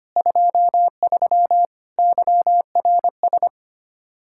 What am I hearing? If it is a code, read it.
Morse code, heard as a keyed tone, about 24 wpm, 700 Hz. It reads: 23 YRS
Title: Speed-Racing Common QSO Phrases - 25wpm